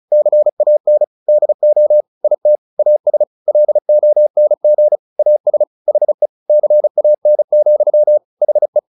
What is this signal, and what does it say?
Morse code, heard as a keyed tone, about 34 wpm, 600 Hz. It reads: CAN DO IT AS LONG AS HE CAN, HE
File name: the old man and the sea / the old man and the sea 1 35wpm